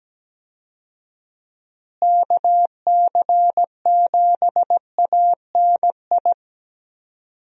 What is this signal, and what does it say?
Morse code, heard as a keyed tone, about 17 wpm, 700 Hz. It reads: KC7ANI
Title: USA Call Signs - 6 Characters - Repeated 3x - 17wpm